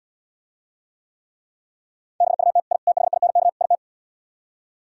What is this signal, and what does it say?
Morse code, heard as a keyed tone, about 77 wpm, 700 Hz. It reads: BUT I THINK IT